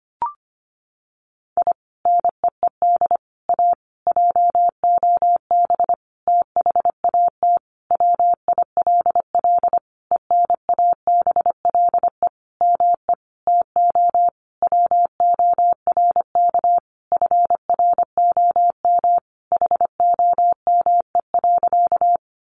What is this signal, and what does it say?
Morse code, heard as a keyed tone, about 25 wpm, 700 Hz. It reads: I NEED A JOB THAT WILL ENABLE ME TO WORK FROM HOME.
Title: Sentences from Top 1500 Words 25wpm